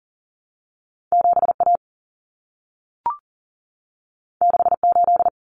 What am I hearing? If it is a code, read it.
Morse code, heard as a keyed tone, about 40 wpm, 700 Hz. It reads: 7A 68
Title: Common Cut Numbers - 2-Digit 40wpm